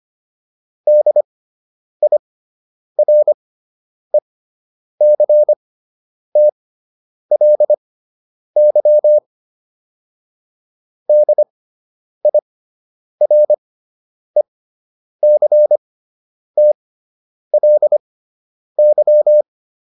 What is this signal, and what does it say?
Morse code, heard as a keyed tone, about 25 wpm, spaced as for 9 wpm, 600 Hz. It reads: DIRECTLY DIRECTLY